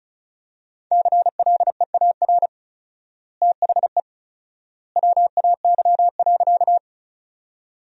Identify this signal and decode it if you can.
Morse code, heard as a keyed tone, about 34 wpm, 700 Hz. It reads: CLEAR THE WAY.
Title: 4x Word-Spacing - Sentences from Top 500 Words - 35wpm